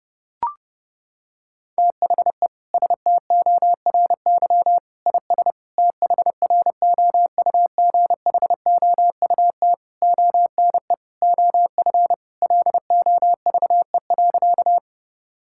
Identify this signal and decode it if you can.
Morse code, heard as a keyed tone, about 32 wpm, 700 Hz. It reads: THE STORY IS THROUGHOUT ONE OF LOVE.
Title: Sentences from Top 1000 Words - Review Part 5 - 30wpm